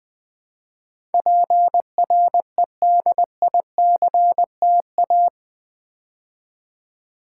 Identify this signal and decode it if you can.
Morse code, heard as a keyed tone, about 20 wpm, 700 Hz. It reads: PREDICTA
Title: Build A Word - 4 Letters At A Time - 11-Letter Words - 20wpm